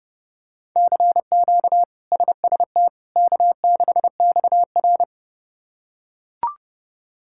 Morse code, heard as a keyed tone, about 32 wpm, 700 Hz. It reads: CQ SST K6XR
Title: K1USN SST QSO Practice - 30wpm